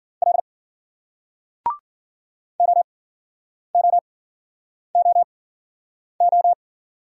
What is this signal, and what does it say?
Morse code, heard as a keyed tone, about 55 wpm, spaced as for 9 wpm, 700 Hz. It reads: L OOOO